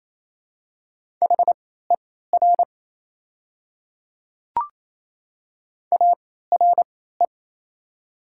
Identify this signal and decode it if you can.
Morse code, heard as a keyed tone, about 28 wpm, spaced as for 16 wpm, 700 Hz. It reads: HER ARE
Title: 3x Character Spacing - Top 100 Words - 28wpm